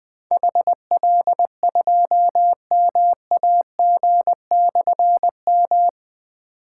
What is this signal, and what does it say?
Morse code, heard as a keyed tone, about 21 wpm, 700 Hz. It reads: HL2MAG/M
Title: Difficult Call Signs 20wpm